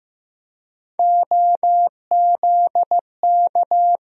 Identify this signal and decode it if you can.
Morse code, heard as a keyed tone, about 15 wpm, 700 Hz. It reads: OZK
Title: USA Call Sign Postfixes - 3 Letters 15wpm